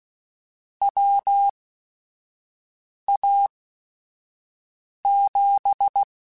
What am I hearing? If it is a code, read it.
Morse code, heard as a keyed tone, about 16 wpm, spaced as for 5 wpm, 800 Hz. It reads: WA7